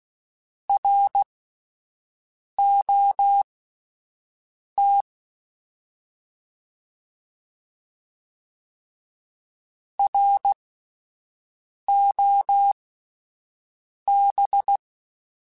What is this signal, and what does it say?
Morse code, heard as a keyed tone, about 16 wpm, spaced as for 5 wpm, 800 Hz. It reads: ROT ROB